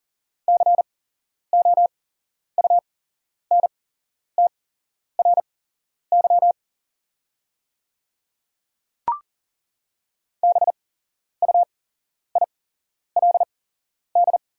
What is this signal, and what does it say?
Morse code, heard as a keyed tone, about 40 wpm, spaced as for 11 wpm, 700 Hz. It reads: COUNTRY BUILD
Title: 8x Character Spacing - Top 500 Words - 40wpm